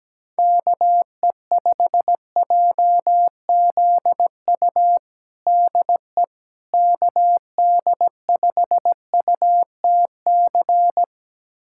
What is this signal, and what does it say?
Morse code, heard as a keyed tone, about 17 wpm, 700 Hz. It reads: KE5JZU DE KD5UTC
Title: USA Callsign DE Callsign 17wpm